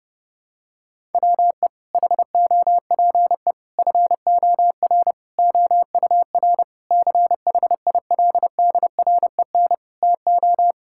Morse code, heard as a keyed tone, about 29 wpm, 700 Hz. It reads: WE HOPE FOR OUR CHILDREN TO
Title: Sentences from Top 1000 Words - Review Part 6 - 30wpm